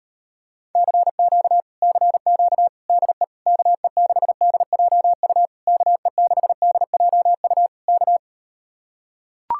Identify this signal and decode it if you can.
Morse code, heard as a keyed tone, about 38 wpm, 700 Hz. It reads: CQ CQ DE KE6DJU KE6DJU K